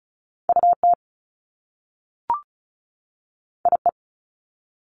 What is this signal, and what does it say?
Morse code, heard as a keyed tone, about 36 wpm, 700 Hz. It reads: UT IE
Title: Top Two Letter Combinations in Words 35wpm